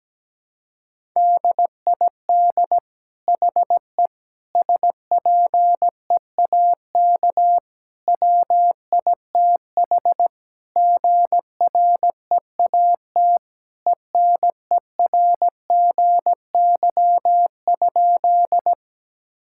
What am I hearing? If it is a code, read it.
Morse code, heard as a keyed tone, about 17 wpm, 700 Hz. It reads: DID HE SPEAK WITH GREAT ENERGY?
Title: Sentences from Top 1000 Words - Review Part 6 - 17wpm